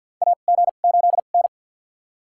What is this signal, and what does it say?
Morse code, heard as a keyed tone, about 48 wpm, spaced as for 35 wpm, 700 Hz. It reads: AG8N
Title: USA Call Signs - 4 Characters - 2x Character Spacing - 50wpm